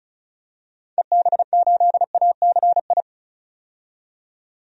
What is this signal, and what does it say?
Morse code, heard as a keyed tone, about 36 wpm, 700 Hz. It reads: EB8ACI
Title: DXCC-100 Call Signs - 6 Characters - 35wpm